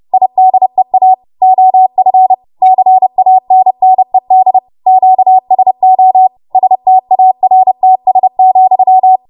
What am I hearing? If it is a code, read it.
Morse code, heard as a keyed tone, about 30 wpm, 750 Hz. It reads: IDEA OF CANNED QSO STARTS,
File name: Morse / 30WPM